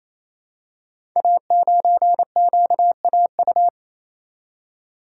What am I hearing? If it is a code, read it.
Morse code, heard as a keyed tone, about 27 wpm, 700 Hz. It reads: A9QAU
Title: Code Groups - 5 Letter:Numbers - 28wpm